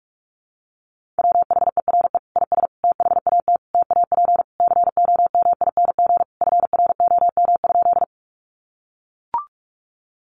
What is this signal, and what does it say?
Morse code, heard as a keyed tone, about 44 wpm, 700 Hz. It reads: WHERE IS THAT TAP COMING FROM?
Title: Sentences from Top 1400 Words 45wpm